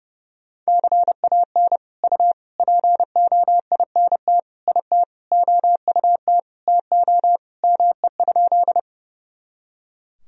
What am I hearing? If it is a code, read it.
Morse code, heard as a keyed tone, about 32 wpm, 700 Hz. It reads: CAN U POINT IT OUT TO ME?
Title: Sentences from Top 200 Words and Abbreviations 30wpm